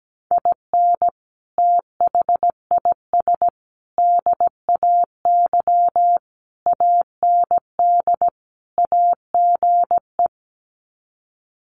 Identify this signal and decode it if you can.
Morse code, heard as a keyed tone, about 17 wpm, 700 Hz. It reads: IN THIS DAY AND AGE
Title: Sets of 5 Words Encore 17wpm